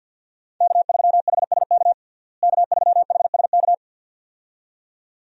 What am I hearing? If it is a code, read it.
Morse code, heard as a keyed tone, about 50 wpm, 700 Hz. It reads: K3HSK K3HSK
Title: USA Call Signs - 5 Characters - Repeated 2x - 50wpm